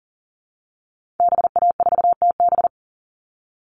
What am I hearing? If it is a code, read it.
Morse code, heard as a keyed tone, about 40 wpm, 700 Hz. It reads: BA4TB
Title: Call Signs Repeated 2x - 40wpm